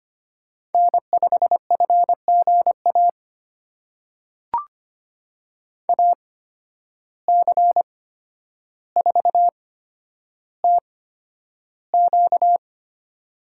Morse code, heard as a keyed tone, about 25 wpm, spaced as for 11 wpm, 700 Hz. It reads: N5FGA AC4TQ
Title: USA Call Signs - 5 Characters - 8x Character Spacing - 25wpm